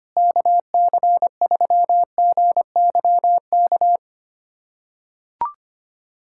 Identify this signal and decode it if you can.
Morse code, heard as a keyed tone, about 25 wpm, 700 Hz. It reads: KC3GYK
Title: USA Callsign DE Callsign 25wpm